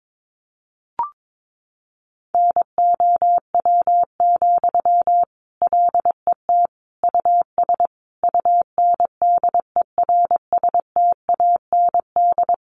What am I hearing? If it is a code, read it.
Morse code, heard as a keyed tone, about 22 wpm, 700 Hz. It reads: NOW, LET US UNDERSTAND
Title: Sentences from Top 1500 Words 22wpm